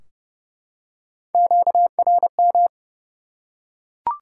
Morse code, heard as a keyed tone, about 30 wpm, 700 Hz. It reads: QRM
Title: Top 180 QSO Elements 30wpm